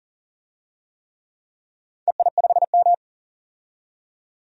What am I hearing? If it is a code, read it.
Morse code, heard as a keyed tone, about 40 wpm, 700 Hz. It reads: EI5M
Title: DXCC-100 Call Signs - 4 Characters - 40wpm